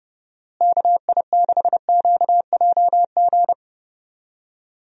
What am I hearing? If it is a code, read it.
Morse code, heard as a keyed tone, about 30 wpm, 700 Hz. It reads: KI6QJG
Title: USA Call Signs - 6 Characters - Repeated 2x - 30wpm